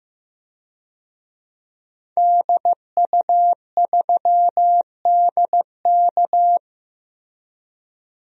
Morse code, heard as a keyed tone, about 15 wpm, 700 Hz. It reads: DU3DK
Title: DXCC-100 Call Signs - 5 Characters - Repeated 3x - 15wpm